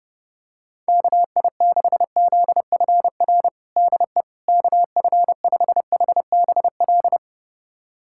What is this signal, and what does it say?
Morse code, heard as a keyed tone, about 29 wpm, 700 Hz. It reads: KI6ZFR DE KF5HBL
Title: USA Callsign DE Callsign - 6 Characters - 30wpm